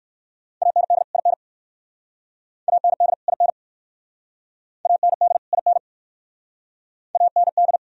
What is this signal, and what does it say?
Morse code, heard as a keyed tone, about 61 wpm, 690 Hz. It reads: AND IN AND IN AND IN AND